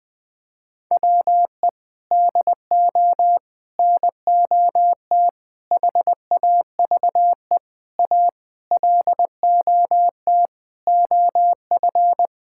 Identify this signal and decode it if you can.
Morse code, heard as a keyed tone, about 20 wpm, 700 Hz. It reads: WE DO NOT HAVE A LOT OF
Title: Sentences with 5-Word Phrases - 20wpm